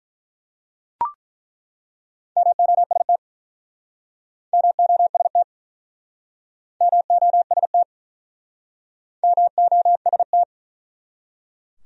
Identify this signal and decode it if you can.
Morse code, heard as a keyed tone, about 49 wpm, 700 Hz. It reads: MOST MOST MOST MOST